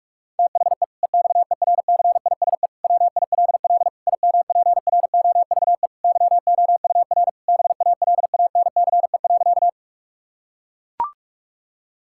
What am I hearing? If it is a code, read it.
Morse code, heard as a keyed tone, about 41 wpm, 700 Hz. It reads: THE EXERCISE WILL IMPROVE YOUR BALANCE.
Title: Sentences from Top 1000 Words - Review Part 6 - 45wpm